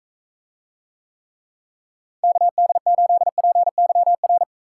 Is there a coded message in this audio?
Yes: Morse code from 2.2 to 4.4 s, heard as a keyed tone, about 42 wpm, 700 Hz. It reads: KD8PYR